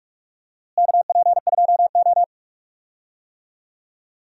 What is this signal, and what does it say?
Morse code, heard as a keyed tone, about 42 wpm, 700 Hz. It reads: KP2O
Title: DXCC-100 Call Signs - 4 Characters - Repeated 3x - 45wpm